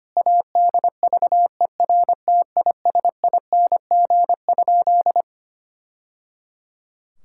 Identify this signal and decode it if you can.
Morse code, heard as a keyed tone, about 25 wpm, 700 Hz. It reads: ADVERTISING?
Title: Sentences from Top 1000 Words - Review Part 6 - 25wpm